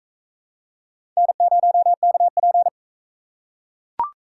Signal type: Morse code

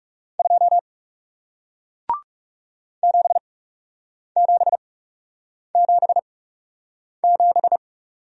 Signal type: Morse code

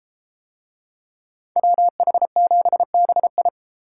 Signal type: Morse code

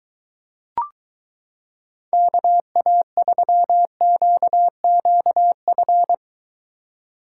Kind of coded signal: Morse code

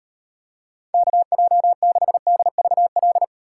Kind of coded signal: Morse code

